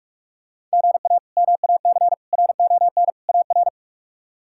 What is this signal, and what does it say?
Morse code, heard as a keyed tone, about 42 wpm, 700 Hz. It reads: GA MAC RON AR
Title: K1USN SST QSO Practice - 45wpm